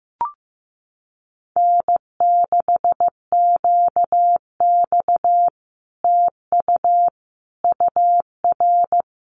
Morse code, heard as a keyed tone, about 15 wpm, 700 Hz. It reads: N6QX TU UR
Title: Common QSO Phrases Encore 15wpm